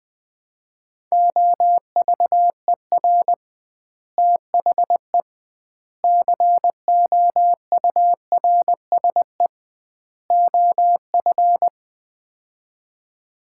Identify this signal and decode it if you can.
Morse code, heard as a keyed tone, about 20 wpm, 700 Hz. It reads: OVER THE COURSE OF
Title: 2x Word-Spacing - Sets of 4 Words - 20wpm